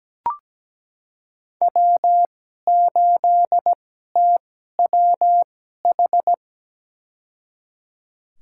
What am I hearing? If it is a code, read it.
Morse code, heard as a keyed tone, about 17 wpm, spaced as for 12 wpm, 700 Hz. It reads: W8TWH
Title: USA Call Signs - 5 Characters - 2x Character Spacing - 17wpm